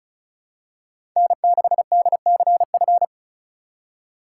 Morse code, heard as a keyed tone, about 34 wpm, 700 Hz. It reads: N6DCF
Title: USA Call Signs - 5 Characters - 4x Character Spacing - 35wpm